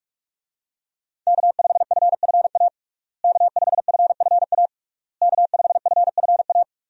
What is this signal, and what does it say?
Morse code, heard as a keyed tone, about 46 wpm, 700 Hz. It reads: K5FFA K5FFA K5FFA
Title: USA Call Signs - 5 Characters - Repeated 3x - 45wpm